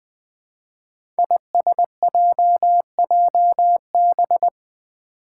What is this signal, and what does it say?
Morse code, heard as a keyed tone, about 20 wpm, 700 Hz. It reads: ISJJB
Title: Five Challenging Letter-Number Combinations 20wpm